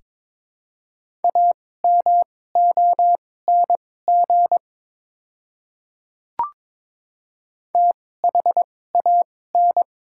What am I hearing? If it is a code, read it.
Morse code, heard as a keyed tone, about 22 wpm, spaced as for 16 wpm, 700 Hz. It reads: AMONG THAN
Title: 2x Character Spacing - Top 500 Words - 22wpm